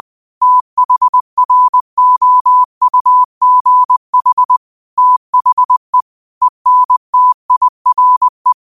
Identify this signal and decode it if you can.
Morse code, heard as a keyed tone, about 20 wpm, 1000 Hz. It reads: THROUGH THE ENTIRE